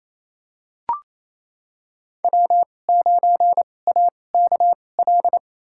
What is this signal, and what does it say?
Morse code, heard as a keyed tone, about 28 wpm, spaced as for 20 wpm, 700 Hz. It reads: W9AKL
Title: USA Call Signs - 5 Characters - 2x Character Spacing - 28wpm